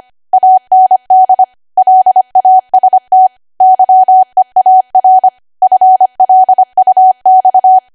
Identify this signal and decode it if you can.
Morse code, heard as a keyed tone, about 25 wpm, 750 Hz. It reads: AND LAST YEAR FLUX
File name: cwa / W1AW-Level3 / 25WPM